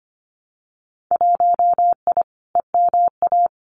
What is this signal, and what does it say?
Morse code, heard as a keyed tone, about 25 wpm, 700 Hz. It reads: 1I EMA
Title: Winter Field Day Exchanges Encore 25wpm